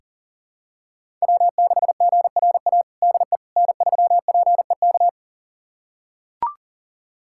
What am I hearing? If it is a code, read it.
Morse code, heard as a keyed tone, about 40 wpm, 700 Hz. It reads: W6GRA DE N3PEK
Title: USA Callsign DE Callsign 40wpm